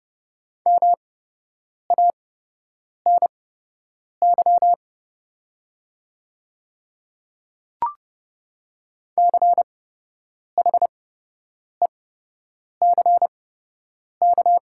Morse code, heard as a keyed tone, about 30 wpm, spaced as for 8 wpm, 700 Hz. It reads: MANY CHECK
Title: 8x Character Spacing - Top 500 Words - 30wpm